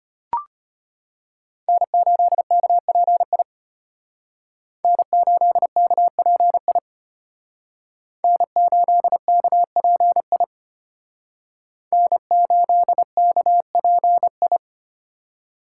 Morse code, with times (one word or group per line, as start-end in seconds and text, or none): 1.7-3.4 N8KPI
4.8-6.8 N8KPI
8.2-10.4 N8KPI
11.9-14.6 N8KPI